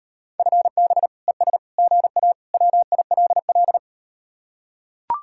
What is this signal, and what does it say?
Morse code, heard as a keyed tone, about 38 wpm, 700 Hz. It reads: FB ES GA WILL